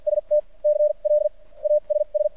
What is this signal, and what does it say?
Morse code, heard as a keyed tone, about 52 wpm, spaced as for 33 wpm, 600 Hz. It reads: ST CLASS
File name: archived / CRSnet